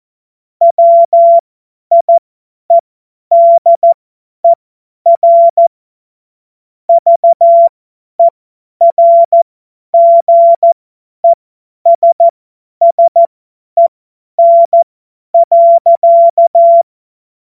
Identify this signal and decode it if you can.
Morse code, heard as a keyed tone, about 14 wpm, spaced as for 10 wpm, 680 Hz. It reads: WIEDER VERGESSEN.
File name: Morsekurs / Witze / Witz 24. Januar 2023 50 F70